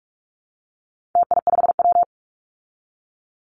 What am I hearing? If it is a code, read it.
Morse code, heard as a keyed tone, about 44 wpm, 700 Hz. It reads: TI5W
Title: Call Signs Repeated 2x - 45wpm